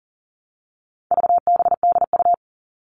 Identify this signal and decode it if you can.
Morse code, heard as a keyed tone, about 40 wpm, 700 Hz. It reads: VBDU
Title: All Cut Numbers - 4-Digit 40wpm